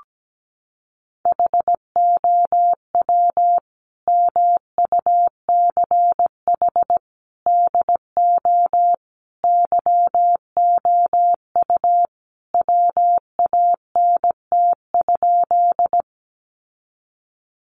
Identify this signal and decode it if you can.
Morse code, heard as a keyed tone, about 17 wpm, 700 Hz. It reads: HOW MUCH DO YOU WANT?
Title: Sentences from Top 200 Words 17wpm